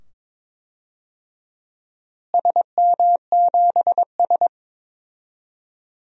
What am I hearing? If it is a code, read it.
Morse code, heard as a keyed tone, about 22 wpm, 700 Hz. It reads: SM7S
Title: DXCC-100 Call Signs - 4 Characters - Repeated 2x - 22wpm